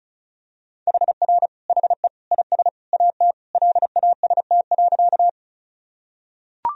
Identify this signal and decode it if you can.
Morse code, heard as a keyed tone, about 36 wpm, 700 Hz. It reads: HR HE IS AT LAST.
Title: Sentences from Top 300 Words and Abbreviations 35wpm